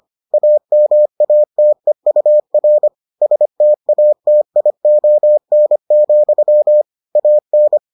Morse code, heard as a keyed tone, about 25 wpm, 600 Hz. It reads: AMATEUR STATION, AN